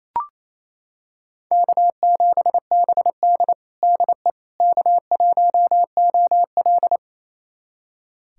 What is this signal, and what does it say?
Morse code, heard as a keyed tone, about 28 wpm, 700 Hz. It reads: K7BD DE K1OL
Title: USA Callsign DE Callsign - 4 Characters - 28wpm